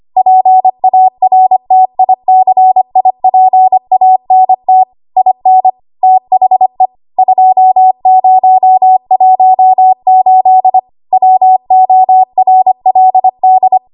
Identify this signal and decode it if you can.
Morse code, heard as a keyed tone, about 25 wpm, 750 Hz. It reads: PARTICIPANT IN THE 2018 WORLD
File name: Archive / 25 WPM / 25WPM